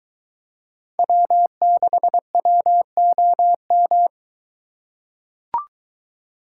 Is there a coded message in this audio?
Yes: Morse code from 1.0 to 4.1 s, heard as a keyed tone, about 23 wpm, 700 Hz. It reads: W6WOM